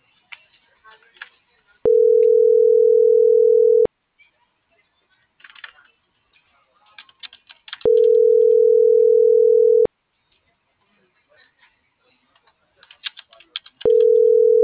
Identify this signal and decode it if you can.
Ringback tone (North America), 440 + 480 Hz, with speech.
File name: CENTRAL TERRESTRE